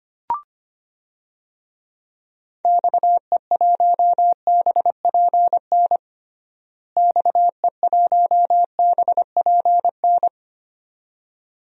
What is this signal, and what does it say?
Morse code, heard as a keyed tone, about 25 wpm, 700 Hz. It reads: XE1BPN XE1BPN
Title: DXCC-100 Call Signs - 6 Characters - Repeated 2x - 25wpm